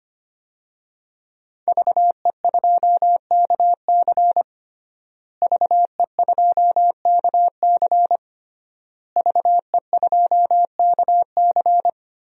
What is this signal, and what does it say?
Morse code, heard as a keyed tone, about 25 wpm, 700 Hz. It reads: VE2KC VE2KC VE2KC
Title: DXCC-100 Call Signs - 5 Characters - Repeated 3x - 25wpm